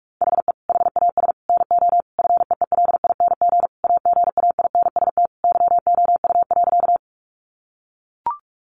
Morse code, heard as a keyed tone, about 44 wpm, 700 Hz. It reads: HE HAS NO FEELING AGAINST YOU.
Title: Sentences from Top 1200 Words 45wpm